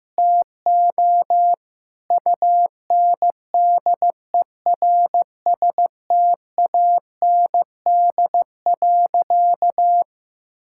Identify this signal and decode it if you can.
Morse code, heard as a keyed tone, about 15 wpm, 700 Hz. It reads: TO UNDERSTAND.
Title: Sentences from Top 1000 Words - Review Part 6 - 15wpm